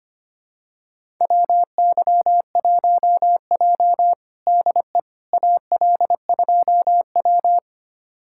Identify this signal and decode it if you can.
Morse code, heard as a keyed tone, about 25 wpm, 700 Hz. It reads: WY1J DE AL2W
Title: USA Callsign DE Callsign - 4 Characters - 25wpm